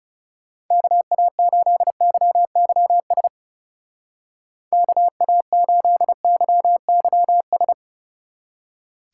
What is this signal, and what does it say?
Morse code, heard as a keyed tone, about 37 wpm, 700 Hz. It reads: KA8YYS KA8YYS
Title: Speed Racing - USA Call Signs 30wpm